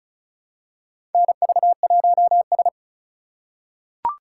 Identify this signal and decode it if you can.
Morse code, heard as a keyed tone, about 36 wpm, 700 Hz. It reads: NV1S
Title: USA Call Signs - 4 Characters - 4x Character Spacing - 35wpm